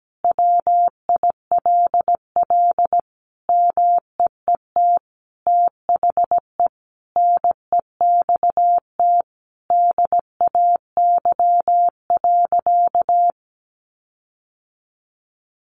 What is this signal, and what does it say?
Morse code, heard as a keyed tone, about 17 wpm, 700 Hz. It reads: WILL MEET THE NEXT DAY.
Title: Sentences from Top 1400 Words 17wpm